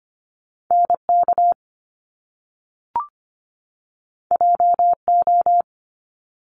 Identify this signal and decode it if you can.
Morse code, heard as a keyed tone, about 25 wpm, 700 Hz. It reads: NK JO
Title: Top Two Letter Combinations in Words 25wpm